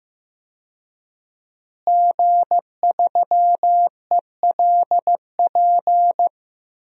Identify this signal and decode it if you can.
Morse code, heard as a keyed tone, about 15 wpm, 700 Hz. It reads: G3ELP
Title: DXCC-100 Call Signs - 5 Characters - Repeated 3x - 15wpm